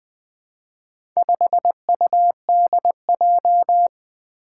Morse code, heard as a keyed tone, about 20 wpm, 700 Hz. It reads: 5UDJ
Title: Four Challenging Letter-Number Combinations 20wpm